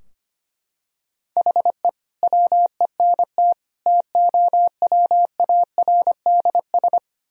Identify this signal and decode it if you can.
Morse code, heard as a keyed tone, about 25 wpm, 700 Hz. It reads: HE WENT TOWARDS
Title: Sentences from Top 1000 Words - Review Part 2 - 25wpm